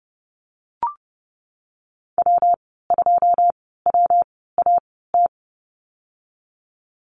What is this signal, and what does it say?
Morse code, heard as a keyed tone, about 30 wpm, spaced as for 17 wpm, 700 Hz. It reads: W2WAT
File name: Lesson 19g - Letter w - Call Signs 30wpm